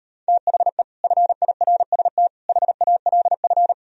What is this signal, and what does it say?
Morse code, heard as a keyed tone, about 38 wpm, 700 Hz. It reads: THE FIRST HALF